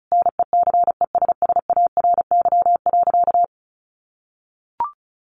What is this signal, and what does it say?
Morse code, heard as a keyed tone, about 34 wpm, 700 Hz. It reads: NECESSARY.
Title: Sentences from Top 1100 Words 35wpm